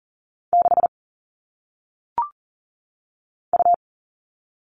Morse code, heard as a keyed tone, about 40 wpm, 700 Hz. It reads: 6 U
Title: All Cut Numbers - 1-Digit 40wpm